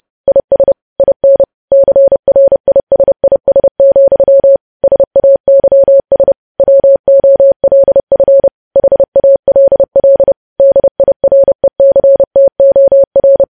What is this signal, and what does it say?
Morse code, heard as a keyed tone, about 30 wpm, 550 Hz. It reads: IS IN CRISIS, SAYS WOLF HALL DIRECTOR
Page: news headlines in morse code at 30 wpm